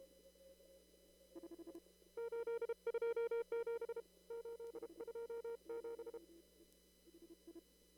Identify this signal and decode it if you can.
Morse code, heard as a keyed tone, about 32 wpm, 460 Hz. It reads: 827 827
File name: CW / kt5j wpx cw-op-k5pi